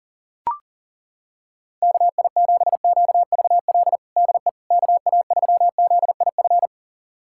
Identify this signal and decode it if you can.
Morse code, heard as a keyed tone, about 37 wpm, 700 Hz. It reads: KI7QVL DE KA3ZIF
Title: USA Callsign DE Callsign 40wpm